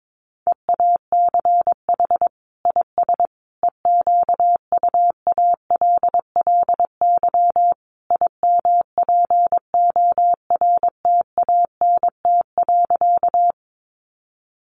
Morse code, heard as a keyed tone, about 22 wpm, 700 Hz. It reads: EACH IS EQUALLY IMPORTANT.
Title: Sentences from Top 1400 Words 22wpm